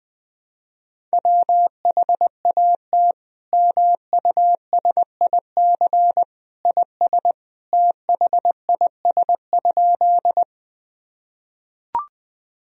Morse code, heard as a keyed tone, about 20 wpm, 700 Hz. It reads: WHAT MUSIC IS THIS?
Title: Sentences from Top 1000 Words - Review Part 1 - 20wpm